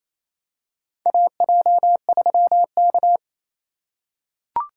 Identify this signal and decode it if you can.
Morse code, heard as a keyed tone, about 28 wpm, 700 Hz. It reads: AJ3K
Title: USA Call Signs - 4 Characters - 2x Character Spacing - 28wpm